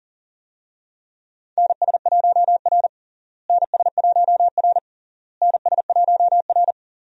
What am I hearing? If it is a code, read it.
Morse code, heard as a keyed tone, about 40 wpm, 700 Hz. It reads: NS1R NS1R NS1R
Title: USA Call Signs - 4 Characters - Repeated 3x - 40wpm